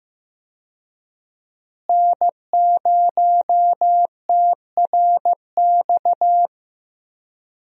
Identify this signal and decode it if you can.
Morse code, heard as a keyed tone, about 15 wpm, 700 Hz. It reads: N0TRX
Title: USA Call Signs - 5 Characters - Repeated 2x - 15wpm